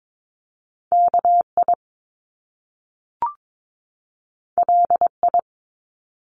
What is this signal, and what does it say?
Morse code, heard as a keyed tone, about 22 wpm, 700 Hz. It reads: KI LI
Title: Top Two Letter Combinations in Words 22wpm